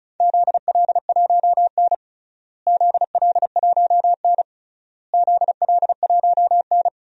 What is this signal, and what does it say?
Morse code, heard as a keyed tone, about 36 wpm, 700 Hz. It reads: ZL1N ZL1N ZL1N
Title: DXCC-100 Call Signs - 4 Characters - Repeated 3x - 35wpm